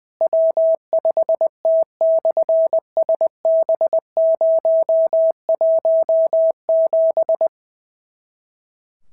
Morse code, heard as a keyed tone, about 20 wpm, 650 Hz. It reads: W5T/SB017
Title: SOTA QSO Practice 20wpm